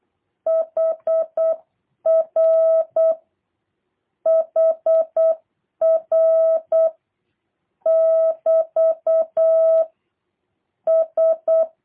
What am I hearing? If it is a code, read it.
Morse code, heard as a keyed tone, about 8 wpm, 660 Hz. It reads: HR HR <BT> S
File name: jh8cbh / 2023cw / review1-3